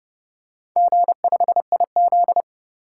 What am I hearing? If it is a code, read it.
Morse code, heard as a keyed tone, about 29 wpm, 700 Hz. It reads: G5IZ